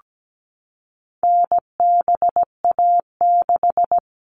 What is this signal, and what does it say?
Morse code, heard as a keyed tone, about 17 wpm, 700 Hz. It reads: NBA6
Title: All Cut Numbers - 4-Digit 17wpm